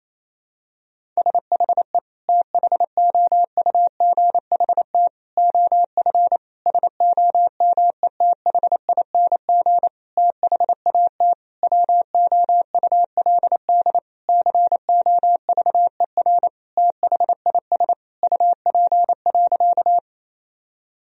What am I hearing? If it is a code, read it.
Morse code, heard as a keyed tone, about 28 wpm, 700 Hz. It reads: SHE THOUGHT OF SOMETHING THAT WOULD COVER THIS UP.
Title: Sentences from Top 1000 Words - Review Part 1 - 28wpm